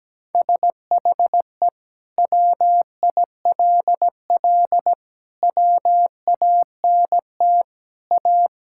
Morse code, heard as a keyed tone, about 17 wpm, 700 Hz. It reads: SHE WILL WANT A
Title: Sentences from Top 1600 Words 17wpm